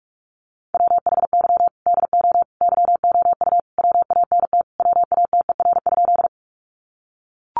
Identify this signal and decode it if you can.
Morse code, heard as a keyed tone, about 44 wpm, 700 Hz. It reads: WHY DO YOU WANT WATER?
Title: Sentences from Top 200 Words 45wpm